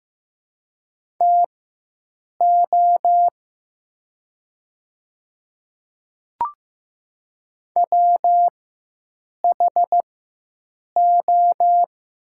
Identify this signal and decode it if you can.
Morse code, heard as a keyed tone, about 15 wpm, spaced as for 7 wpm, 700 Hz. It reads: TO WHO
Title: 4x Character Spacing - Top 100 Words - 15wpm